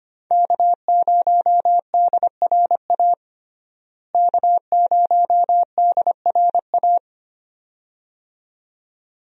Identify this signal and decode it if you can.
Morse code, heard as a keyed tone, about 25 wpm, 700 Hz. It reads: K0DRA K0DRA
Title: USA Call Signs - 5 Characters - Repeated 2x - 25wpm